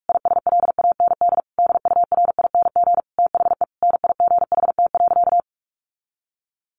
Morse code, heard as a keyed tone, about 44 wpm, 700 Hz. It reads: ISLAND DURING THE NIGHT.
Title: Sentences from Top 1400 Words 45wpm